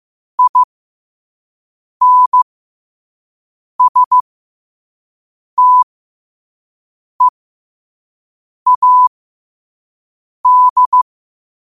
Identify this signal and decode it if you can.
Morse code, heard as a keyed tone, about 15 wpm, spaced as for 5 wpm, 1000 Hz. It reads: INSTEAD